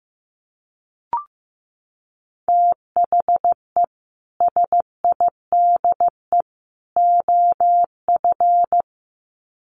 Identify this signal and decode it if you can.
Morse code, heard as a keyed tone, about 15 wpm, 700 Hz. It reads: THE SIDE OF
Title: Sets of 3 Words 15wpm